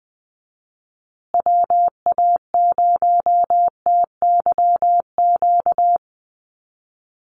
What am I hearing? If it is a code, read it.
Morse code, heard as a keyed tone, about 20 wpm, 700 Hz. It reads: WA0TYQ
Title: USA Call Signs Repeated 2x 20wpm